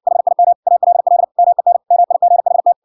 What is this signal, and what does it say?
Morse code, heard as a keyed tone, about 63 wpm, 700 Hz. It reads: HIM ALL MEN MIGHT